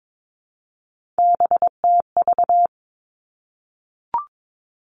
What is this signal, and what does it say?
Morse code, heard as a keyed tone, about 22 wpm, 700 Hz. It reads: BTV
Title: Three Challenging Letter-Number Combinations 22wpm